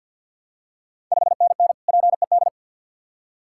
Morse code, heard as a keyed tone, about 51 wpm, 690 Hz. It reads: 5NN PED
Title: Ontario QSO Party Exchange 50wpm